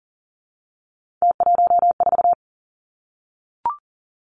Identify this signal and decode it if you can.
Morse code, heard as a keyed tone, about 37 wpm, 700 Hz. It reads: T14
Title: Three Challenging Letter-Number Combinations 40wpm